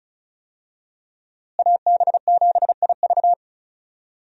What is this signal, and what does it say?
Morse code, heard as a keyed tone, about 36 wpm, 700 Hz. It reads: AB7IV
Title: USA Call Signs - 5 Characters - Repeated 3x - 35wpm